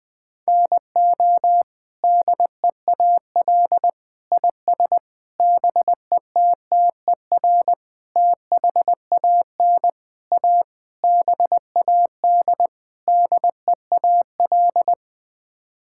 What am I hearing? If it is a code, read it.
Morse code, heard as a keyed tone, about 20 wpm, 700 Hz. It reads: NO DEAL IS BETTER THAN A BAD DEAL
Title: Sets of 8 Words - 20wpm